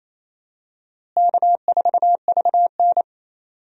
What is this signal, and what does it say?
Morse code, heard as a keyed tone, about 28 wpm, 700 Hz. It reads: K4VN
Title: USA Call Signs - 4 Characters - Repeated 2x - 28wpm